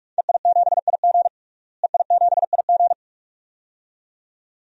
Morse code, heard as a keyed tone, about 46 wpm, 700 Hz. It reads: EI7IG EI7IG
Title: DXCC-100 Call Signs - 5 Characters - Repeated 2x - 45wpm